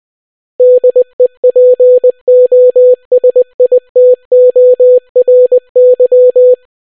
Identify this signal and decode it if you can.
Morse code, heard as a keyed tone, about 20 wpm, 500 Hz. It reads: DEPOSITORY